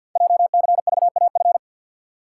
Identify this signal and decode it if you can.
Morse code, heard as a keyed tone, about 52 wpm, 700 Hz. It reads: JCVAF
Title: Code Groups - 5 Letters - 50wpm